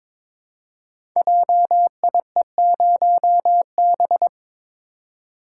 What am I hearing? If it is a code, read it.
Morse code, heard as a keyed tone, about 22 wpm, 700 Hz. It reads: JIE0B
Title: Code Groups - 5 Letter:Numbers - 22wpm